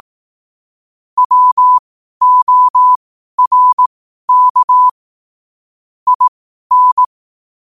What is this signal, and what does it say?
Morse code, heard as a keyed tone, about 18 wpm, spaced as for 12 wpm, 1000 Hz. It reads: WORK IN
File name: qotdfiles / 12wpm